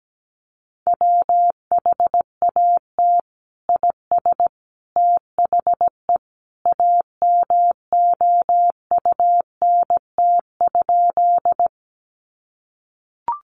Morse code, heard as a keyed tone, about 17 wpm, 700 Hz. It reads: WHAT IS THE AMOUNT?
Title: Sentences from Top 700 Words 17wpm